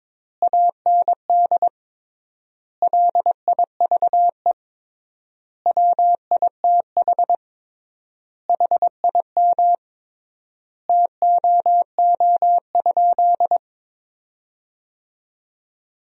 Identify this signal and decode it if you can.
Morse code, heard as a keyed tone, about 22 wpm, 700 Hz. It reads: AND LIVE WITH HIM TOO?
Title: 3x Word-Spacing - Sentences from Top 500 Words - 22wpm